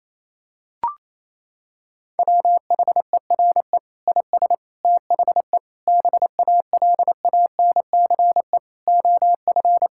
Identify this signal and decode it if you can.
Morse code, heard as a keyed tone, about 28 wpm, 700 Hz. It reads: WHERE IS THE BALANCE OF
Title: Sentences from Top 1000 Words - Review Part 6 - 28wpm